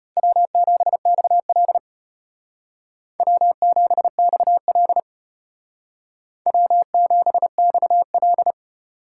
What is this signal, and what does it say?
Morse code, heard as a keyed tone, about 31 wpm, 700 Hz. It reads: W7XL W7XL W7XL